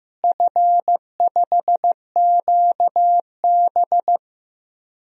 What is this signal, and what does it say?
Morse code, heard as a keyed tone, about 15 wpm, 700 Hz. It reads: F5QB
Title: DXCC-100 Call Signs - 4 Characters - Repeated 3x - 15wpm